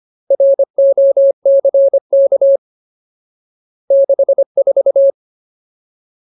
Morse code, heard as a keyed tone, about 25 wpm, 550 Hz. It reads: ROCK 64
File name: Ultimate Word List / 25WPM Ultimate